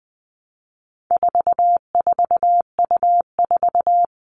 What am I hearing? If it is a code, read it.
Morse code, heard as a keyed tone, about 20 wpm, 700 Hz. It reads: 44U4
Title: All Cut Numbers - 4-Digit 20wpm